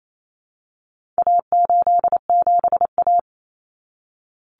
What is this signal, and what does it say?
Morse code, heard as a keyed tone, about 28 wpm, 700 Hz. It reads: A87A
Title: Common Cut Numbers - 4-Digit 28wpm